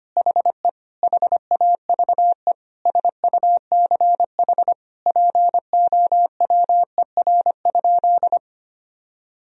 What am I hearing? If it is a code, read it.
Morse code, heard as a keyed tone, about 25 wpm, 700 Hz. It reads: HE HAVE SUCH POWER?
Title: Sentences from Top 1000 Words - Review Part 2 - 25wpm